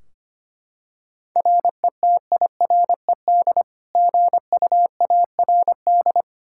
Morse code, heard as a keyed tone, about 25 wpm, 700 Hz. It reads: RETIRED GUARD
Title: K1USN SST QSO Practice 25wpm